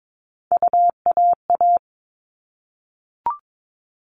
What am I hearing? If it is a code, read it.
Morse code, heard as a keyed tone, about 22 wpm, 700 Hz. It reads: UAA
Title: All Cut Numbers - 3-Digit 22wpm